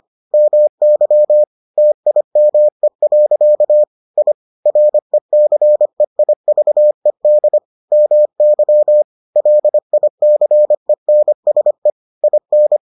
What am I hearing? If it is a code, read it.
Morse code, heard as a keyed tone, about 25 wpm, 600 Hz. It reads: MY TIME. I RECEIVED MY LICENSE IN